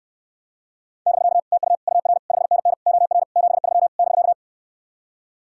Morse code, heard as a keyed tone, about 69 wpm, 700 Hz. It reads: <BK> TU UR 5NN OR 73 <BK>